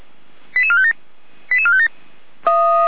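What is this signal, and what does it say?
Five-tone selective call (ZVEI), heard as quick runs of five tones, tones of about 70 ms. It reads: 88457 88457
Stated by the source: ZVEI five-tone selective call received on an SDR.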